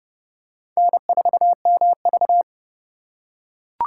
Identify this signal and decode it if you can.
Morse code, heard as a keyed tone, about 28 wpm, 700 Hz. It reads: N4MV